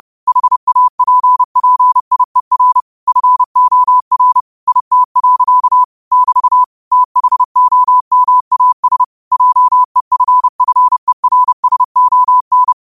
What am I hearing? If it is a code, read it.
Morse code, heard as a keyed tone, about 30 wpm, 1000 Hz. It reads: HAPPIER FOR IT. <BT> THOMAS JEFFERSON
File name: qotdfiles / 30wpm